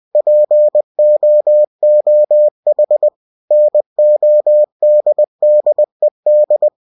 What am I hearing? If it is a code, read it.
Morse code, heard as a keyed tone, about 20 wpm, 600 Hz. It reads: POOH NODDED
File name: cw-books / winnie-the-pooh / winnie-20wpm-page-0034